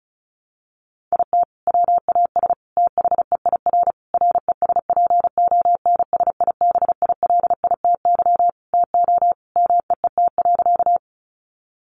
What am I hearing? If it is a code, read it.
Morse code, heard as a keyed tone, about 34 wpm, 700 Hz. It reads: IT WAS THEIR RESPONSIBILITY TO MEET.
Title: Sentences from Top 1200 Words 35wpm